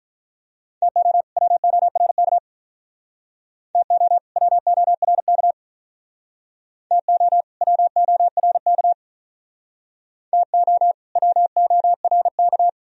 Morse code, heard as a keyed tone, about 46 wpm, 700 Hz. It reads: TO WORK TO WORK TO WORK TO WORK